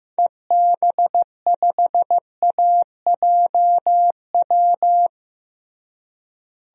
Morse code, heard as a keyed tone, about 15 wpm, 700 Hz. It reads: EB5AJW
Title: DXCC-100 Call Signs - 6 Characters - Repeated 3x - 15wpm